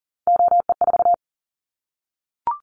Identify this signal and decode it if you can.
Morse code, heard as a keyed tone, about 42 wpm, 700 Hz. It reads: OE4
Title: Callsign Prefixes for Top 100 DXCC - 40wpm